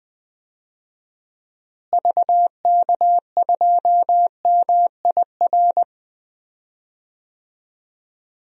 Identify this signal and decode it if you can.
Morse code, heard as a keyed tone, about 20 wpm, 700 Hz. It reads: VK2MIR
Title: DXCC-100 Call Signs - 6 Characters - Repeated 2x - 20wpm